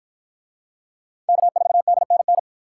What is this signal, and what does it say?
Morse code, heard as a keyed tone, about 53 wpm, 700 Hz. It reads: K4DNN